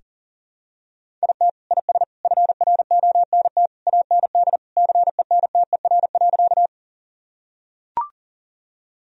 Morse code, heard as a keyed tone, about 40 wpm, 700 Hz. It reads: IT IS FRONT AND CENTER.
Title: Sentences from Top 1000 Words - Review Part 2 - 40wpm